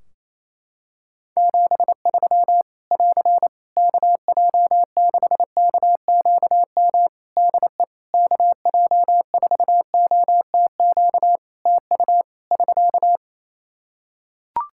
Morse code, heard as a keyed tone, about 27 wpm, 700 Hz. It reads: 73 <AR> KJ6KQM DE KJ4OTQ TU <SK>
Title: Practice QSO - Exchange 4 - Ending - 28wpm